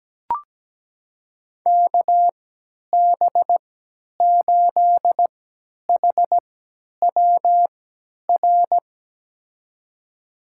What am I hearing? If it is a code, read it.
Morse code, heard as a keyed tone, about 17 wpm, spaced as for 10 wpm, 700 Hz. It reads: KB8HWR
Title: USA Call Signs - 6 Characters - 3x Character Spacing - 17wpm